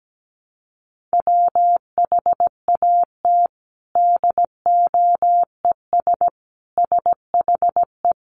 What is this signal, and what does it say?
Morse code, heard as a keyed tone, about 17 wpm, 700 Hz. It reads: WHAT DOES SHE
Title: Sentences from Top 200 Words 17wpm